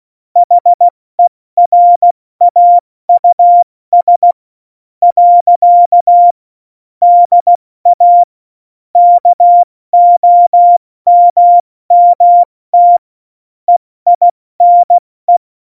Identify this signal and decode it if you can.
Morse code, heard as a keyed tone, about 16 wpm, spaced as for 13 wpm, 700 Hz. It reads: HERAUS. DA KOMMT EINE
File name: Morsekurs / Witze / Witz 18. September 2018, 70 F80